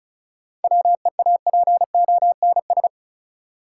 Morse code, heard as a keyed tone, about 36 wpm, 700 Hz. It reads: WEAPONS
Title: Sets of 5 Words Encore - New Word Drill - 35wpm